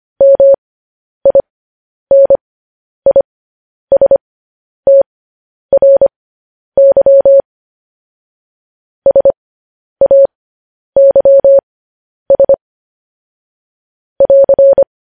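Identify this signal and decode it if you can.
Morse code, heard as a keyed tone, about 25 wpm, spaced as for 10 wpm, 550 Hz. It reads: MINISTRY SAYS <AR>
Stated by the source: Morse code news headlines at 10 wpm.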